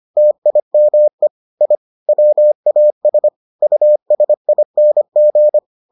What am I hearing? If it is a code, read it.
Morse code, heard as a keyed tone, about 25 wpm, 600 Hz. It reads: TIME I WAS USING